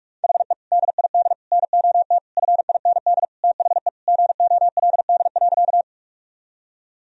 Morse code, heard as a keyed tone, about 46 wpm, 700 Hz. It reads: HE DID NOT FIND THE GOLD.
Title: Sentences from Top 1000 Words - Review Part 3 - 45wpm